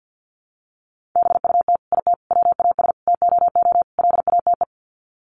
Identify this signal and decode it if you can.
Morse code, heard as a keyed tone, about 50 wpm, 700 Hz. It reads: BUT IT WAS TOO LATE
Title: Sets of 5 Words Encore2 50wpm